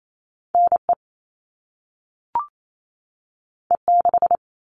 Morse code, heard as a keyed tone, about 28 wpm, 700 Hz. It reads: NE E6
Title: All Cut Numbers - 2-Digit 28wpm